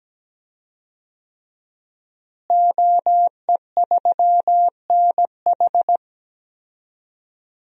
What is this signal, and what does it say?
Morse code, heard as a keyed tone, about 17 wpm, 700 Hz. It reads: OE3NH
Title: DXCC-100 Call Signs - 5 Characters - Repeated 2x - 17wpm